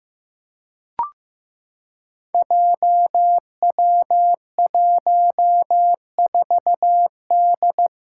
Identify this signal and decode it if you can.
Morse code, heard as a keyed tone, about 15 wpm, 700 Hz. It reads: JW14D
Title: Five Challenging Letter-Number Combinations 15wpm